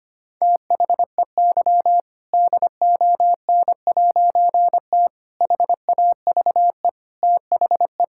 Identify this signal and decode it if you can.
Morse code, heard as a keyed tone, about 25 wpm, 700 Hz. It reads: THEY DON'T HAVE THE
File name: practice / 25wpm / All-Age Appropriate Jokes - 25wpm